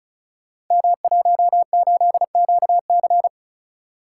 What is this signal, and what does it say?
Morse code, heard as a keyed tone, about 36 wpm, 700 Hz. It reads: M18QC
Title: Code Groups - 5 Letter:Numbers - 35wpm